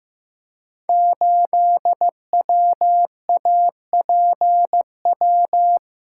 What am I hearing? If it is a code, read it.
Morse code, heard as a keyed tone, about 15 wpm, 700 Hz. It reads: 8WAPW
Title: Code Groups - 5 Letter:Numbers - 15wpm